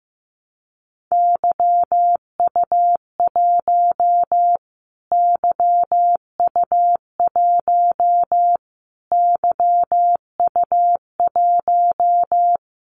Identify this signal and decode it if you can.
Morse code, heard as a keyed tone, about 15 wpm, 700 Hz. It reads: YU1 YU1 YU1
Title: Callsign Prefixes for Top 25 DXCC - Repeated 3x - 15wpm